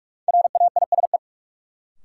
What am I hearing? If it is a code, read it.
Morse code, heard as a keyed tone, about 44 wpm, 700 Hz. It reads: RAISE
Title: Sentences from Top 1400 Words - New Word Drill - 45wpm